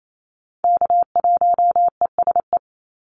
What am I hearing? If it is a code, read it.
Morse code, heard as a keyed tone, about 28 wpm, 700 Hz. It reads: K1ESE
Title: Call Signs - 28wpm